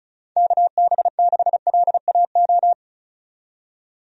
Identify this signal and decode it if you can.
Morse code, heard as a keyed tone, about 36 wpm, 700 Hz. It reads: KB6LAO
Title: USA Call Signs - 6 Characters - 35wpm